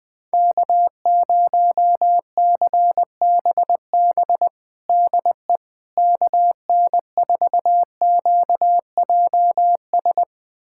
Morse code, heard as a keyed tone, about 20 wpm, 700 Hz. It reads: K0CBB DE KN4QJS
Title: USA Callsign DE Callsign 20wpm